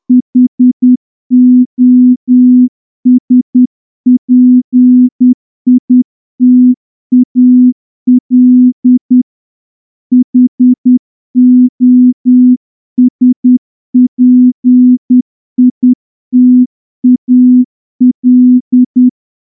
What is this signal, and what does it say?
Morse code, heard as a keyed tone, about 10 wpm, 250 Hz. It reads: HOSPITAL HOSPITAL
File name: morse-training-10WPM